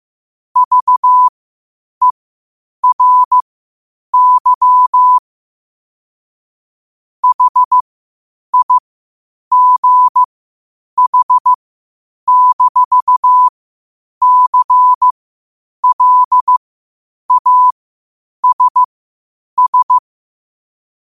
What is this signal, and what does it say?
Morse code, heard as a keyed tone, about 15 wpm, spaced as for 8 wpm, 1000 Hz. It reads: VERY HIGH-CLASS
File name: qotdfiles / 8wpm